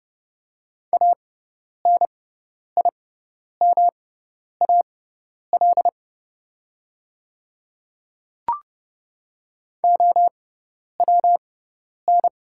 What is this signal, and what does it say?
Morse code, heard as a keyed tone, about 30 wpm, spaced as for 10 wpm, 700 Hz. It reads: ANIMAL OWN
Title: 6x Character Spacing - Top 300 Words - 30wpm